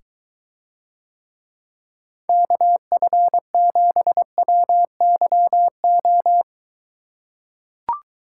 Morse code, heard as a keyed tone, about 22 wpm, 700 Hz. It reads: KF7WYO